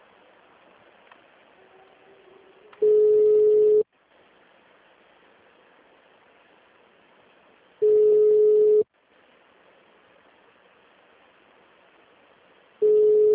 Ringback tone (Europe, 425 Hz).
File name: out- -2001- - - .494